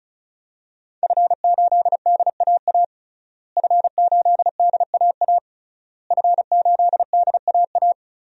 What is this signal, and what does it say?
Morse code, heard as a keyed tone, about 34 wpm, 700 Hz. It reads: F8DAA F8DAA F8DAA
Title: DXCC-100 Call Signs - 5 Characters - Repeated 3x - 35wpm